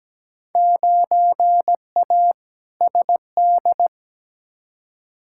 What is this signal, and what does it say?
Morse code, heard as a keyed tone, about 17 wpm, 700 Hz. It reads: 9A SD
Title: Field Day Exchanges Encore - 17wpm